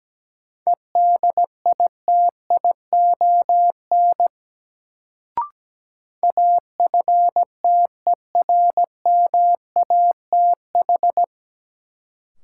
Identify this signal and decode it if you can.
Morse code, heard as a keyed tone, about 17 wpm, 700 Hz. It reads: EDITION AFTERMATH
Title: Sets of 5 Words - New Word Drill - 17wpm